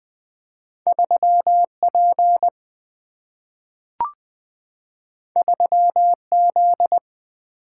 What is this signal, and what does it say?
Morse code, heard as a keyed tone, about 20 wpm, 700 Hz. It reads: 3P 3Z
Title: Single-Letter Single-Number 20wpm